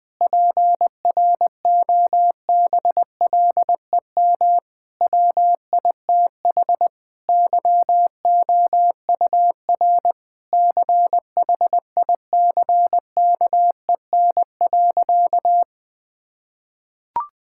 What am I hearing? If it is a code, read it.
Morse code, heard as a keyed tone, about 21 wpm, 700 Hz. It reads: PROBLEM WITH YOUR CHICKEN.
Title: Sentences from Top 1000 Words - Review Part 5 - 20wpm